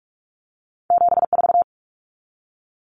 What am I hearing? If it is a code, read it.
Morse code, heard as a keyed tone, about 46 wpm, 700 Hz. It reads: 74
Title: Common Cut Numbers - 2-Digit 45wpm